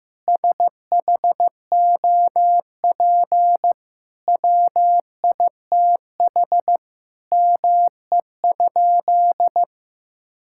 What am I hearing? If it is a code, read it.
Morse code, heard as a keyed tone, about 15 wpm, 700 Hz. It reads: SHOP WITH ME?
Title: Sentences from Top 1600 Words 15wpm